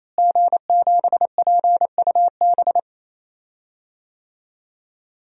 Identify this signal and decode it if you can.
Morse code, heard as a keyed tone, about 28 wpm, 700 Hz. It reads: G7PUB
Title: DXCC-100 Call Signs - 5 Characters - Repeated 3x - 28wpm